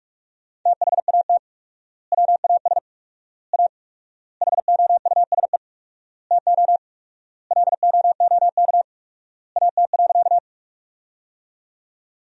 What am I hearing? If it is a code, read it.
Morse code, heard as a keyed tone, about 46 wpm, 700 Hz. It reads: THAT WAS A HOUSE TO LOOK AT.
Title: 4x Word-Spacing - Sentences from Top 200 Words - 45wpm